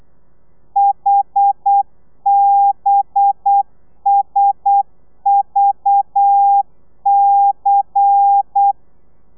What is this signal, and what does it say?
Morse code, heard as a keyed tone, about 8 wpm, 800 Hz. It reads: HBSVC
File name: telegrafia / i2tfj / Lez10 35